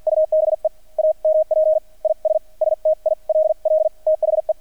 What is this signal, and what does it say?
Morse code, heard as a keyed tone, about 50 wpm, spaced as for 42 wpm, 630 Hz. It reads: UBE AMP IS STILL THE
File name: archived / CRSnet